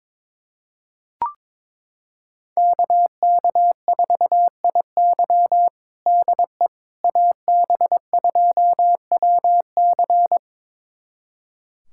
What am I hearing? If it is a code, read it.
Morse code, heard as a keyed tone, about 22 wpm, 700 Hz. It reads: KK4IY DE AB2WC
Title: USA Callsign DE Callsign - 5 Characters - 22wpm